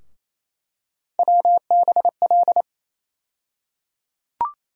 Morse code, heard as a keyed tone, about 28 wpm, 700 Hz. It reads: WBL
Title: USA Call Sign Postfixes - 3 Letters 28wpm